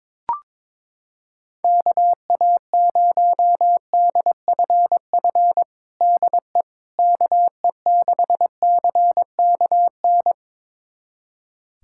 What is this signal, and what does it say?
Morse code, heard as a keyed tone, about 22 wpm, 700 Hz. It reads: KA0DFF DE KE6CKN
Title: USA Callsign DE Callsign 22wpm